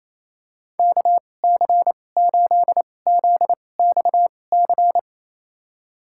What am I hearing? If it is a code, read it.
Morse code, heard as a keyed tone, about 28 wpm, spaced as for 20 wpm, 700 Hz. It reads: KC8ZXC
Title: USA Call Signs - 6 Characters - 2x Character Spacing - 28wpm